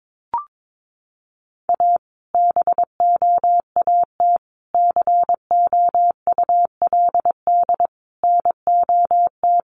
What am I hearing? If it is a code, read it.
Morse code, heard as a keyed tone, about 22 wpm, 700 Hz. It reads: A BOAT COULD NOT
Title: Sentences from Top 1400 Words 22wpm